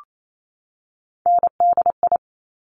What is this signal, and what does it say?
Morse code, heard as a keyed tone, about 28 wpm, 700 Hz. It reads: NDI
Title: Top Three Letter Combinations in Words 28wpm